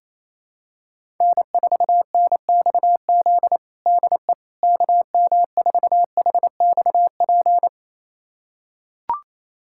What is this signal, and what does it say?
Morse code, heard as a keyed tone, about 28 wpm, 700 Hz. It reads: N4NXZ DE KM4HXP
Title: USA Callsign DE Callsign 28wpm